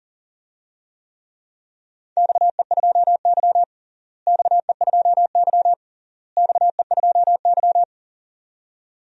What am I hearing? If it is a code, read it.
Morse code, heard as a keyed tone, about 37 wpm, 700 Hz. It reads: XE2Y XE2Y XE2Y
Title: DXCC-100 Call Signs - 4 Characters - Repeated 3x - 40wpm